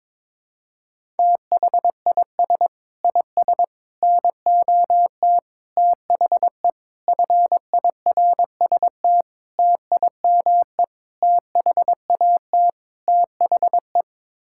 Morse code, heard as a keyed tone, about 22 wpm, 700 Hz. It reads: THIS IS NOT THE FIRST TIME THAT THE
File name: practice / 22wpm / Sets of 8 Words - 22wpm